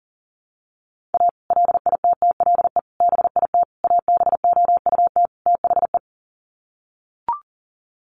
Morse code, heard as a keyed tone, about 38 wpm, 700 Hz. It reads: A LITTLE BIT ABOUT THE
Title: Sets of 5 Words Encore2 40wpm